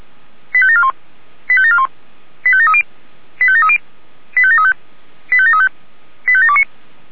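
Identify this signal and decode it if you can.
Five-tone selective call (ZVEI), heard as quick runs of five tones, tones of about 70 ms. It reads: 85741 85741 85730 85730 85735 85735 85729
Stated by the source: ZVEI five-tone call received by SDR.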